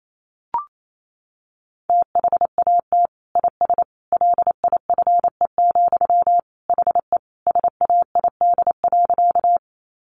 Morse code, heard as a keyed tone, about 28 wpm, 700 Hz. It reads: THAT IS LIFE, HE SAID.
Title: Sentences from Top 900 Words 28wpm